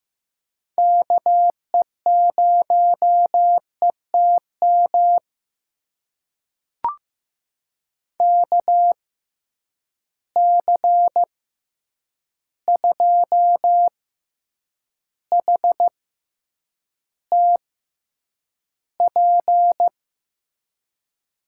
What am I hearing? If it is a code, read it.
Morse code, heard as a keyed tone, about 15 wpm, spaced as for 8 wpm, 700 Hz. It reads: KE0ETM KC2HTP
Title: USA Call Signs - 6 Characters - 6x Character Spacing - 15wpm